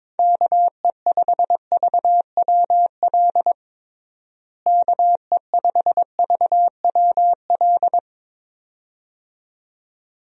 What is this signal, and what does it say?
Morse code, heard as a keyed tone, about 22 wpm, 700 Hz. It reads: KE5VWL KE5VWL
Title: USA Call Signs - 6 Characters - Repeated 2x - 22wpm